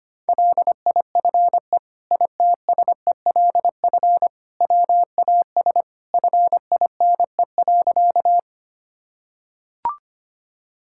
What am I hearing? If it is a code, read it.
Morse code, heard as a keyed tone, about 25 wpm, 700 Hz. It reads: LIFE ITSELF WAS FINE.
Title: Sentences from Top 1000 Words - Review Part 5 - 25wpm